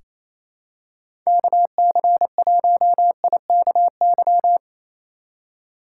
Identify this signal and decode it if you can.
Morse code, heard as a keyed tone, about 28 wpm, 700 Hz. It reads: KC1IKY
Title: USA Call Signs - 6 Characters - 2x Character Spacing - 28wpm